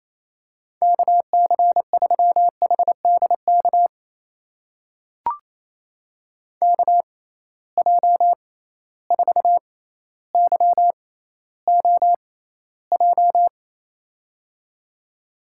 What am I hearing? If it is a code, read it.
Morse code, heard as a keyed tone, about 28 wpm, spaced as for 14 wpm, 700 Hz. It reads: KC3HDK KJ4YOJ
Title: USA Call Signs - 6 Characters - 6x Character Spacing - 28wpm